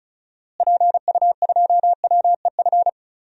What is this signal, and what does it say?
Morse code, heard as a keyed tone, about 36 wpm, 700 Hz. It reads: PU2WEF
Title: DXCC-100 Call Signs - 6 Characters - Repeated 3x - 35wpm